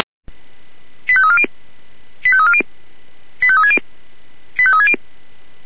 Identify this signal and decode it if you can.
Five-tone selective call (ZVEI), heard as quick runs of five tones, tones of about 70 ms. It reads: 85340 85340 85360 85360
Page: SDR/ZVEI